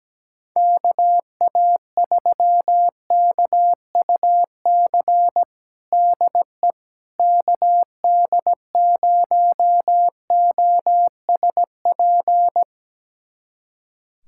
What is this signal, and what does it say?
Morse code, heard as a keyed tone, about 17 wpm, 700 Hz. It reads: KA3KUC DE KD0OSP
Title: USA Callsign DE Callsign 17wpm